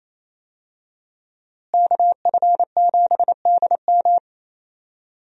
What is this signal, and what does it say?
Morse code, heard as a keyed tone, about 28 wpm, 700 Hz. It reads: KF7DM
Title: USA Call Signs - 5 Characters - Repeated 3x - 28wpm